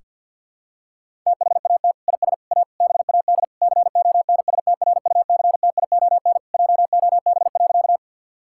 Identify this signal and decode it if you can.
Morse code, heard as a keyed tone, about 51 wpm, 700 Hz. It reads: THAT IS A BAD CONSTRUCTION JOB.
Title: Sentences from Top 1600 Words 50wpm